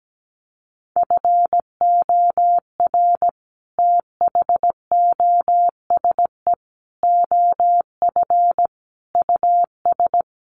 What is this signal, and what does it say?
Morse code, heard as a keyed tone, about 17 wpm, 700 Hz. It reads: FOR THOSE OF US
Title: Sets of 5 Words Encore 17wpm